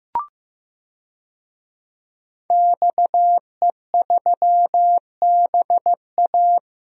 Morse code, heard as a keyed tone, about 15 wpm, 700 Hz. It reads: XE3BA
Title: DXCC-100 Call Signs - 5 Characters - Repeated 3x - 15wpm